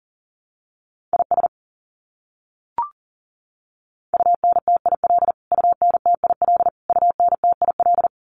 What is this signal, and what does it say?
Morse code, heard as a keyed tone, about 40 wpm, 700 Hz. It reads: IS UNTIL UNTIL UNTIL
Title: Top 500 Words Repeated 40wpm